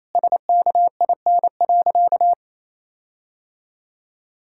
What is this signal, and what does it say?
Morse code, heard as a keyed tone, about 28 wpm, 700 Hz. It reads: SKIN.
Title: Sentences from Top 1600 Words 28wpm